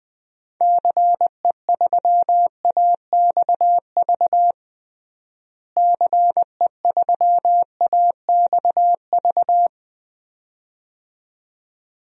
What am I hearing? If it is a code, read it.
Morse code, heard as a keyed tone, about 20 wpm, 700 Hz. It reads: CE3AXV CE3AXV
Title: DXCC-100 Call Signs - 6 Characters - Repeated 2x - 20wpm